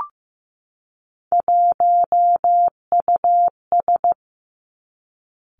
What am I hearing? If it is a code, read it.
Morse code, heard as a keyed tone, about 15 wpm, 700 Hz. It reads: 1US
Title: Three Challenging Letter-Number Combinations 15wpm